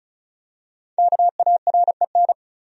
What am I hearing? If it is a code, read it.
Morse code, heard as a keyed tone, about 35 wpm, 700 Hz. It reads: KAREN
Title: USA Common Names - Max 5 Letters - 35wpm